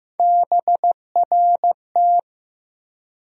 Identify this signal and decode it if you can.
Morse code, heard as a keyed tone, about 15 wpm, 700 Hz. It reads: BRT
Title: USA Call Sign Postfixes - 3 Letters 15wpm